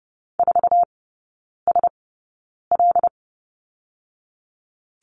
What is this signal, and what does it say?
Morse code, heard as a keyed tone, about 30 wpm, spaced as for 9 wpm, 700 Hz. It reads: 4SL
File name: 081 Lesson 14e - Number 2 - Three Characters 30wpm